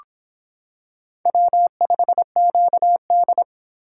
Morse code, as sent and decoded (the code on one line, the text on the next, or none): .-- ..... --.- -..
W5QD